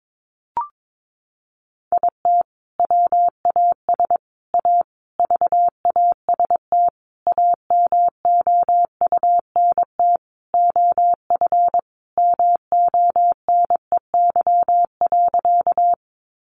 Morse code, heard as a keyed tone, about 22 wpm, 700 Hz. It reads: IT WAS A VAST AMOUNT OF MONEY.
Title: Sentences from Top 1300 Words 22wpm